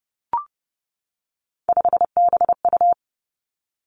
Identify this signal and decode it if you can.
Morse code, heard as a keyed tone, about 30 wpm, 700 Hz. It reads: 5BU
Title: Three Challenging Letter-Number Combinations 30wpm